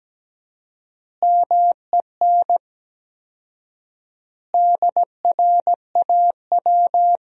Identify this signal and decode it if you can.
Morse code, heard as a keyed tone, about 17 wpm, 700 Hz. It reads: MEN DRAW
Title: 4x Word-Spacing - Sentences from Top 300 Words - 17wpm